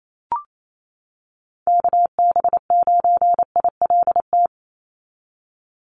Morse code, heard as a keyed tone, about 28 wpm, 700 Hz. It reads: KB9ILT
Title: NA SOTA Activator Callsigns 28wpm